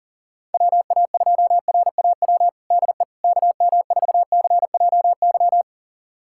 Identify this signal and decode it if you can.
Morse code, heard as a keyed tone, about 40 wpm, 700 Hz. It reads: WA2RAW DE KM4CJY
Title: USA Callsign DE Callsign - 6 Characters - 40wpm